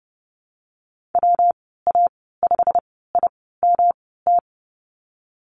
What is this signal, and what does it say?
Morse code, heard as a keyed tone, about 31 wpm, 700 Hz. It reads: W A 5 I M T
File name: Lesson 19g - Letter w - Call Signs 30wpm